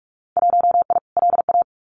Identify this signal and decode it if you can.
Morse code, heard as a keyed tone, about 46 wpm, 700 Hz. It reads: 1I LA
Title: Winter Field Day Exchanges Encore 45wpm